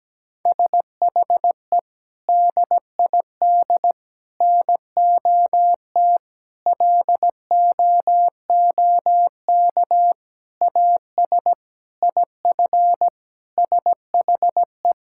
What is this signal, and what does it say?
Morse code, heard as a keyed tone, about 17 wpm, 700 Hz. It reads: SHE DID NOT LOOK AS IF SHE
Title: Sentences from Top 1000 Words - Review Part 6 - 17wpm